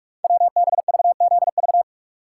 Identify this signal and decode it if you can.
Morse code, heard as a keyed tone, about 44 wpm, 700 Hz. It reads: WBVZV
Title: Code Groups - 5 Letters - 45wpm